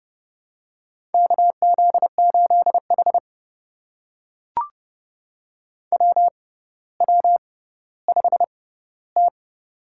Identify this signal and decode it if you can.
Morse code, heard as a keyed tone, about 30 wpm, spaced as for 15 wpm, 700 Hz. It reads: KZ8H WW5T
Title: USA Call Signs - 4 Characters - 6x Character Spacing - 30wpm